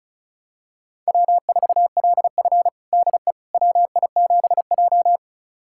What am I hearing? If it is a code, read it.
Morse code, heard as a keyed tone, about 36 wpm, 700 Hz. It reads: W4LF DE WI7J
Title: USA Callsign DE Callsign - 4 Characters - 35wpm